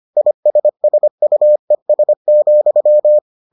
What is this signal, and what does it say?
Morse code, heard as a keyed tone, about 25 wpm, 600 Hz. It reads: ISSUES,